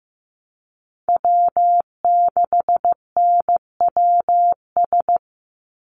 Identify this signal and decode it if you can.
Morse code, heard as a keyed tone, about 15 wpm, 700 Hz. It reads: W6NWS
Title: Call Signs Repeated 2x - 15wpm